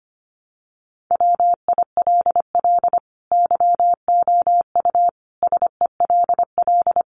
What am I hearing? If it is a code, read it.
Morse code, heard as a keyed tone, about 25 wpm, 700 Hz. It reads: WILL YOU SELL
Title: Sentences from Top 1200 Words 25wpm